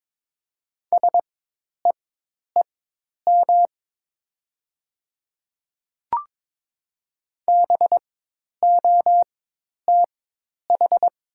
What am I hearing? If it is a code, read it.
Morse code, heard as a keyed tone, about 22 wpm, spaced as for 10 wpm, 700 Hz. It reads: SEEM BOTH
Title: 4x Character Spacing - Top 300 Words - 22wpm